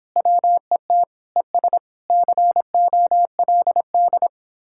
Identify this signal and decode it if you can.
Morse code, heard as a keyed tone, about 29 wpm, 700 Hz. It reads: WET ES COLD